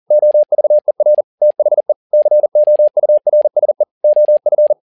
Morse code, heard as a keyed tone, about 40 wpm, 600 Hz. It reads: OVER THE COURSE OF
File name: learning center / tower trouble / tower touble 40wpm